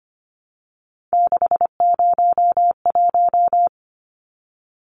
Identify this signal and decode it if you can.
Morse code, heard as a keyed tone, about 25 wpm, 700 Hz. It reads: 601
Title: Numbers - 3-Digit 25wpm